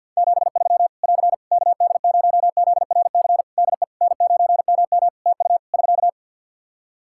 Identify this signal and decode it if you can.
Morse code, heard as a keyed tone, about 50 wpm, 700 Hz. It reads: 73 <AR> KD0ZRC DE N9MM TU <SK>
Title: Practice QSO - Exchange 4 - Ending - 50wpm